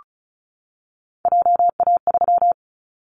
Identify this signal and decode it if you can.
Morse code, heard as a keyed tone, about 36 wpm, 700 Hz. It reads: JA3
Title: Callsign Prefixes for Top 25 DXCC - 35wpm